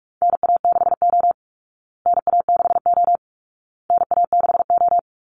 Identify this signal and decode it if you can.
Morse code, heard as a keyed tone, about 44 wpm, 700 Hz. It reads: NA6O NA6O NA6O
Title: Call Signs Repeated 2x - 45wpm